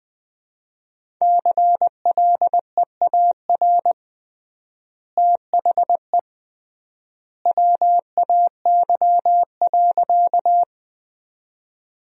Morse code, heard as a keyed tone, about 21 wpm, 700 Hz. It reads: CLEAR THE WAY.
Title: 3x Word-Spacing - Sentences from Top 500 Words - 20wpm